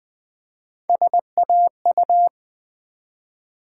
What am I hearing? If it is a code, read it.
Morse code, heard as a keyed tone, about 20 wpm, 700 Hz. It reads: SAU
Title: Build A Word - 1 Letter At A Time - 5-Letter Words - 20wpm